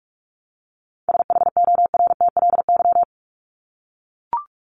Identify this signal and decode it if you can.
Morse code, heard as a keyed tone, about 44 wpm, 700 Hz. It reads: SHORTLY
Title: LY Suffix - Top 100 Words with Base Words 45wpm